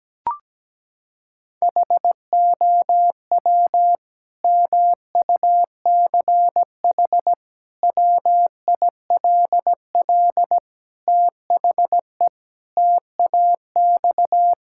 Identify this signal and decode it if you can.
Morse code, heard as a keyed tone, about 17 wpm, 700 Hz. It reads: HOW MUCH WILL THE TAX
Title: Sentences from Top 1000 Words - Review Part 5 - 17wpm